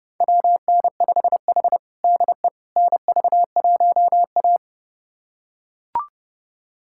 Morse code, heard as a keyed tone, about 32 wpm, 700 Hz. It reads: WN5H DE NV1A
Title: USA Callsign DE Callsign - 4 Characters - 30wpm